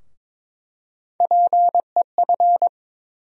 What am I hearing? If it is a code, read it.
Morse code, heard as a keyed tone, about 22 wpm, 700 Hz. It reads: PEF
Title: USA Call Sign Postfixes - 3 Letters 22wpm